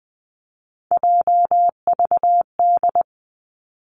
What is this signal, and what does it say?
Morse code, heard as a keyed tone, about 20 wpm, 700 Hz. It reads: JVD
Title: Three Challenging Letter-Number Combinations 20wpm